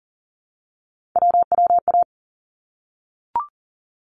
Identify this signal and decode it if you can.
Morse code, heard as a keyed tone, about 40 wpm, 700 Hz. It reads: WWA
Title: ARRL Sections 40wpm